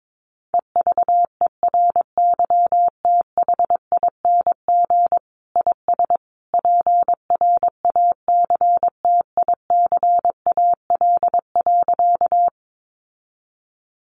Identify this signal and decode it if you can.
Morse code, heard as a keyed tone, about 22 wpm, 700 Hz. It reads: EVERYTHING IS PRACTICAL.
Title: Sentences from Top 1400 Words 22wpm